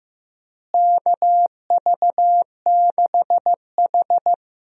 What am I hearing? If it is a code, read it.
Morse code, heard as a keyed tone, about 15 wpm, 700 Hz. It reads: KV6H
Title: USA Call Signs - 4 Characters - Repeated 3x - 15wpm